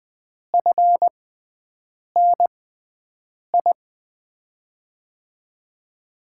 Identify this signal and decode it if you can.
Morse code, heard as a keyed tone, about 20 wpm, spaced as for 7 wpm, 700 Hz. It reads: FNI
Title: USA Call Signs - 5 Characters - 6x Character Spacing - 20wpm